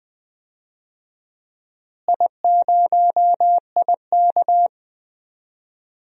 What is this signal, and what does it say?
Morse code, heard as a keyed tone, about 20 wpm, 700 Hz. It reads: I0IK
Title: DXCC-100 Call Signs - 4 Characters - Repeated 2x - 20wpm